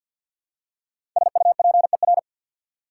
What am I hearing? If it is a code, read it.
Morse code, heard as a keyed tone, about 54 wpm, 690 Hz. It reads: SUPER
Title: Prefixes and Suffixes 50wpm